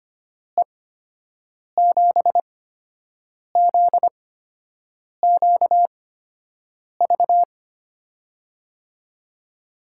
Morse code, heard as a keyed tone, about 25 wpm, spaced as for 7 wpm, 700 Hz. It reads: E7ZQV
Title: USA Call Signs - 6 Characters - 8x Character Spacing - 25wpm